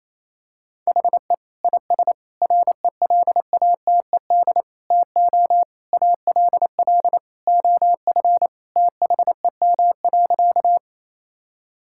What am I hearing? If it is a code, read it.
Morse code, heard as a keyed tone, about 28 wpm, 700 Hz. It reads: HE IS RELATED TO ALL OF THEM.
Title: Sentences from Top 1000 Words - Review Part 5 - 28wpm